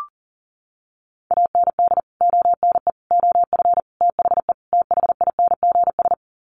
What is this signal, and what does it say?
Morse code, heard as a keyed tone, about 40 wpm, 700 Hz. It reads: AND ONE OF THE THINGS
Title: Sets of 5 Words 40wpm